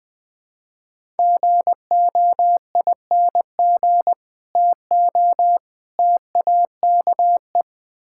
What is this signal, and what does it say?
Morse code, heard as a keyed tone, about 20 wpm, 700 Hz. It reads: GOING TO TAKE
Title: Build A Phrase - 3-Word Phrases - 20wpm